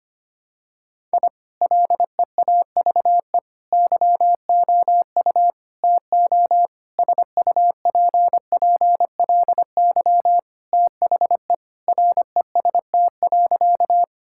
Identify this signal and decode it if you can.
Morse code, heard as a keyed tone, about 25 wpm, 700 Hz. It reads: I LEAVE YOU TO SUPPLY THE REST.
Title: Sentences from Top 1000 Words - Review Part 5 - 25wpm